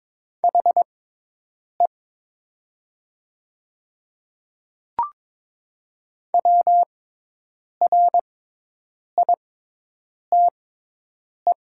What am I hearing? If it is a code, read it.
Morse code, heard as a keyed tone, about 22 wpm, spaced as for 8 wpm, 700 Hz. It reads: HE WRITE
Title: 6x Character Spacing - Top 100 Words - 22wpm